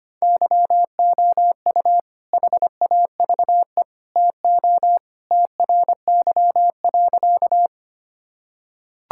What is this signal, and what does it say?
Morse code, heard as a keyed tone, about 25 wpm, 700 Hz. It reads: YOU HAVE TO TRY.
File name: practice / 25wpm / Sentences with 3-Word Phrases - 25wpm